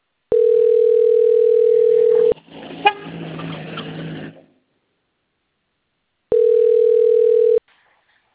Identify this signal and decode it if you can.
Ringback tone (North America), 440 + 480 Hz.